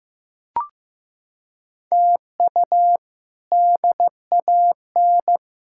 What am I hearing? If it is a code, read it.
Morse code, heard as a keyed tone, about 16 wpm, 700 Hz. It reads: TU DAN
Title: K1USN SST QSO Practice - 15wpm